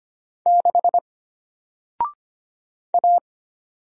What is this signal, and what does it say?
Morse code, heard as a keyed tone, about 25 wpm, 700 Hz. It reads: 6 A
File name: Single-Letter-Number-25wpm